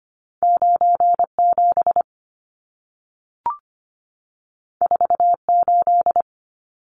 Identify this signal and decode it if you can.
Morse code, heard as a keyed tone, about 25 wpm, 700 Hz. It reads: 97 48
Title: Numbers - 2-Digit 25wpm